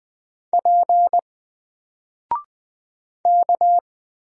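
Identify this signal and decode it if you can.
Morse code, heard as a keyed tone, about 20 wpm, 700 Hz. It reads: P K
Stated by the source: Single letter and number Morse drill at 20 wpm, from N4LCD.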